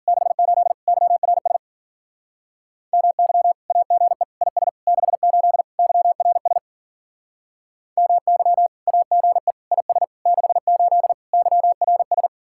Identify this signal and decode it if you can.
Morse code, heard as a keyed tone, about 49 wpm, 700 Hz. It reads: 68 YRS MY AGE IS 68 YRS MY AGE IS 68 YRS
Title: Speed-Racing Common QSO Phrases - 40wpm